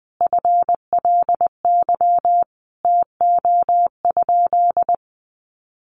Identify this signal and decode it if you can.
Morse code, heard as a keyed tone, about 20 wpm, 700 Hz. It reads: FLY TO?
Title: Sentences from Top 500 Words 20wpm